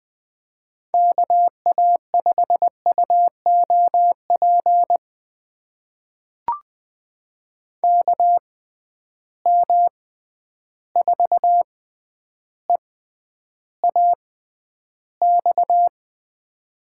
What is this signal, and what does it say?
Morse code, heard as a keyed tone, about 20 wpm, spaced as for 10 wpm, 700 Hz. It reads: KA5UOP KM4EAX
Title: USA Call Signs - 6 Characters - 6x Character Spacing - 20wpm